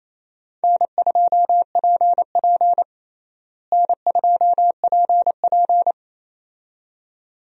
Morse code, heard as a keyed tone, about 28 wpm, 700 Hz. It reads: N2PP N2PP
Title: USA Call Signs - 4 Characters - Repeated 2x - 28wpm